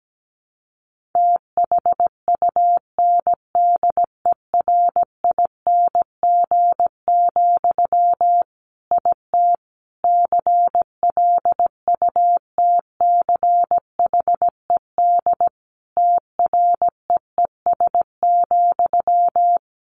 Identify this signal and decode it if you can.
Morse code, heard as a keyed tone, about 17 wpm, 700 Hz. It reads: THUNDERING, IT CLUTCHED TREES,
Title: A Sound Of Thunder 17wpm